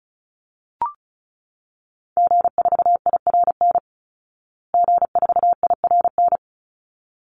Morse code, heard as a keyed tone, about 34 wpm, 700 Hz. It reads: G4IRN G4IRN
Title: Call Signs Repeated 3x - 35wpm